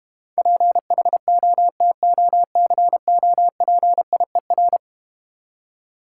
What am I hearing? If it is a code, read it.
Morse code, heard as a keyed tone, about 31 wpm, 700 Hz. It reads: PHOTOCOPIER
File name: Easy Words - Speed-Racing with Hints - 25wpm